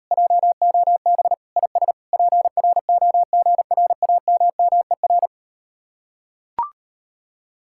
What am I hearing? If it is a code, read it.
Morse code, heard as a keyed tone, about 38 wpm, 700 Hz. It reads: JOB IS PROGRAMMER